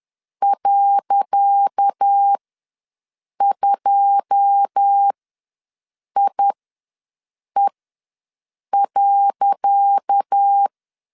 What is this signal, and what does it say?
Morse code, heard as a keyed tone, about 10 wpm, spaced as for 6 wpm, 800 Hz. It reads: .2IE.
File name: LEZIONI / 10 SIGNS / 10C DOT ws 10wpm 4213